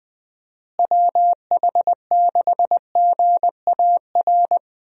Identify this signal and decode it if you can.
Morse code, heard as a keyed tone, about 20 wpm, 700 Hz. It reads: WH6GAR
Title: USA Call Signs - 6 Characters - Repeated 3x - 20wpm